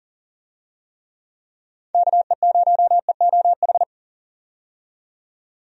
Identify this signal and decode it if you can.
Morse code, heard as a keyed tone, about 40 wpm, 700 Hz. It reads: KE0EOH
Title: USA Call Signs - 6 Characters - 40wpm